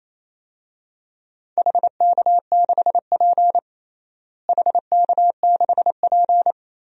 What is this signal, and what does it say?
Morse code, heard as a keyed tone, about 28 wpm, 700 Hz. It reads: HK6P HK6P
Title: DXCC-100 Call Signs - 4 Characters - Repeated 3x - 28wpm